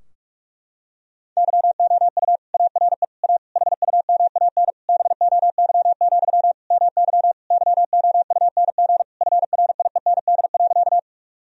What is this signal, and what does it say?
Morse code, heard as a keyed tone, about 42 wpm, 700 Hz. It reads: YOU ARE A HUMAN BOY, MY YOUNG FRIEND.
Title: Sentences from Top 1000 Words - Review Part 3 - 45wpm